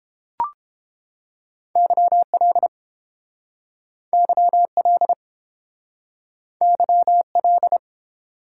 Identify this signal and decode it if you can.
Morse code, heard as a keyed tone, about 29 wpm, 700 Hz. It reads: YL YL YL